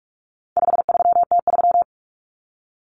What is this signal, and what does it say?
Morse code, heard as a keyed tone, about 44 wpm, 700 Hz. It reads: 53T3
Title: Common Cut Numbers - 4-Digit 45wpm